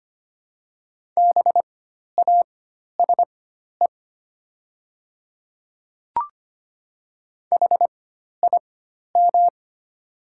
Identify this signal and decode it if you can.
Morse code, heard as a keyed tone, about 25 wpm, spaced as for 12 wpm, 700 Hz. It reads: BASE HIM
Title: 4x Character Spacing - Top 300 Words - 25wpm